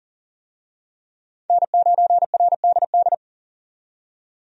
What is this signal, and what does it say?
Morse code, heard as a keyed tone, about 40 wpm, 700 Hz. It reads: N9RDD
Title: USA Call Signs - 5 Characters - 40wpm